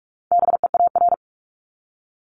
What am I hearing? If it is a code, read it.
Morse code, heard as a keyed tone, about 46 wpm, 700 Hz. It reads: BEAR
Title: Multi-letter Phonemes Repeated 3x 45wpm